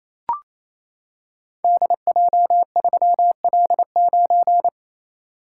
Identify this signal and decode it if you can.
Morse code, heard as a keyed tone, about 29 wpm, 700 Hz. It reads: DJ3L9
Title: Code Groups - 5 Letter:Numbers - 28wpm